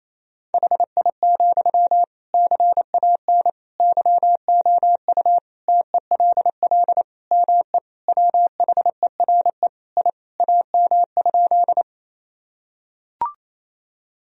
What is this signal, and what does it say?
Morse code, heard as a keyed tone, about 28 wpm, 700 Hz. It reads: HI, CAN YOU TELL ME WHERE I AM?
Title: Sentences from Top 1000 Words - Review Part 3 - 28wpm